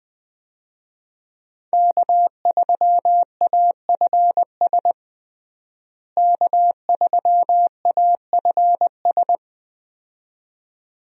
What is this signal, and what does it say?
Morse code, heard as a keyed tone, about 20 wpm, 700 Hz. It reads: K3AFS K3AFS
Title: USA Call Signs - 5 Characters - Repeated 2x - 20wpm